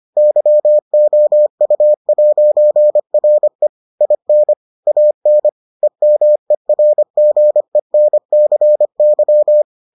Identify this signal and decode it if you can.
Morse code, heard as a keyed tone, about 25 wpm, 600 Hz. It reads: YOU'RE IN AN EMERGENCY